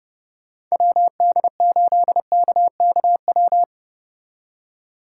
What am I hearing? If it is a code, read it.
Morse code, heard as a keyed tone, about 36 wpm, 700 Hz. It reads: WD8KKW
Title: Speed Racing - USA 2x3 Call Signs 30wpm